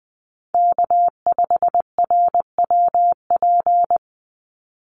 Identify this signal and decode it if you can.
Morse code, heard as a keyed tone, about 20 wpm, 700 Hz. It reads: K5RWP
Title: NA SOTA Chaser Callsigns 20wpm